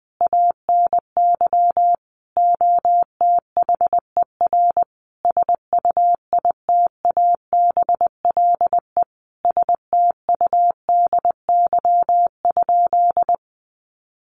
Morse code, heard as a keyed tone, about 20 wpm, 700 Hz. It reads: ANY OTHER SUITABLE STUDY?
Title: Sentences from Top 1400 Words 20wpm